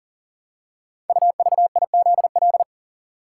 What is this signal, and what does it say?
Morse code, heard as a keyed tone, about 38 wpm, 700 Hz. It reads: UVIZL
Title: Code Groups - 5 Letters - 40wpm